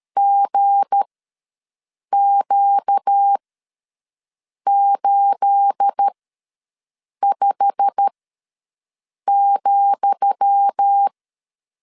Morse code, heard as a keyed tone, about 13 wpm, spaced as for 6 wpm, 800 Hz. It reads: GQ85,
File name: LEZIONI / 10 SIGNS / 10D COMA ws 12wpm 3618